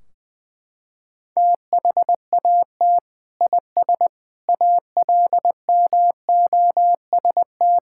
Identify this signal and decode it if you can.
Morse code, heard as a keyed tone, about 21 wpm, 700 Hz. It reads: THAT IS ALMOST
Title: Sentences from Top 1600 Words 20wpm